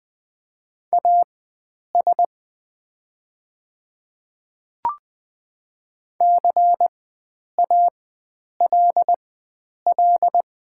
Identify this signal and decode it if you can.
Morse code, heard as a keyed tone, about 20 wpm, spaced as for 9 wpm, 700 Hz. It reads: AS CALL
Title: 4x Character Spacing - Top 300 Words - 20wpm